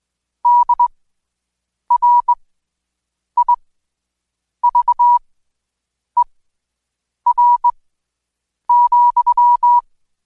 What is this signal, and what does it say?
Morse code, heard as a keyed tone, about 21 wpm, spaced as for 7 wpm, 970 Hz. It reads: DRIVER,
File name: Code Quick / AudioDownld2 / Seg112-Day17-DayDrill 17